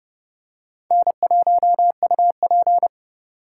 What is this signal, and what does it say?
Morse code, heard as a keyed tone, about 32 wpm, 700 Hz. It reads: N1UP
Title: USA Call Signs - 4 Characters - 3x Character Spacing - 30wpm